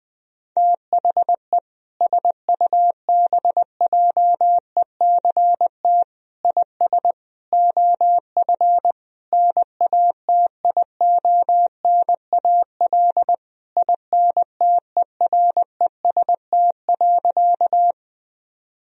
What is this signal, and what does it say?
Morse code, heard as a keyed tone, about 20 wpm, 700 Hz. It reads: THE SUBJECT IS OF NATIONAL INTEREST.
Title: Sentences from Top 1000 Words - Review Part 4 - 20wpm